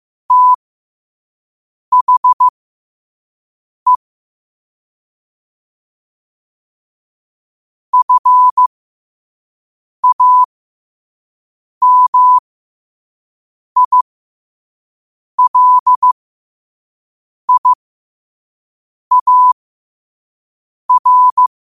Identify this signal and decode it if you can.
Morse code, heard as a keyed tone, about 15 wpm, spaced as for 5 wpm, 1000 Hz. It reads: THE FAMILIAR
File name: 5wpm